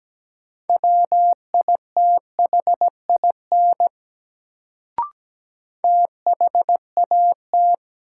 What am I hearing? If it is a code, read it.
Morse code, heard as a keyed tone, about 17 wpm, 700 Hz. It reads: WITHIN THAT
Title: Sets of 2 Words - New Word Drill - 17wpm